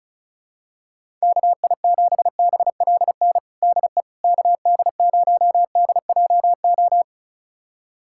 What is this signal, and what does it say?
Morse code, heard as a keyed tone, about 34 wpm, 700 Hz. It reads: KI7BLN DE KD0DJO
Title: USA Callsign DE Callsign 35wpm